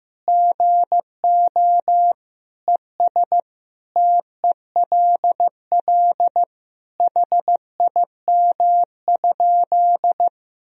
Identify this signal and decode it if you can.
Morse code, heard as a keyed tone, about 16 wpm, 700 Hz. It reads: GO ES TELL HIM?
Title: Sentences from Top 200 Words and Abbreviations 15wpm